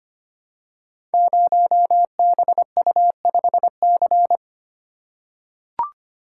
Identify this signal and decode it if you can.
Morse code, heard as a keyed tone, about 25 wpm, 700 Hz. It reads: 0BU5C
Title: Code Groups - 5 Letter:Numbers - 25wpm